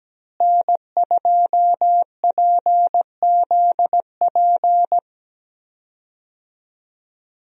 Morse code, heard as a keyed tone, about 17 wpm, 700 Hz. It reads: N2PZP
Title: USA Call Signs - 5 Characters - Repeated 2x - 17wpm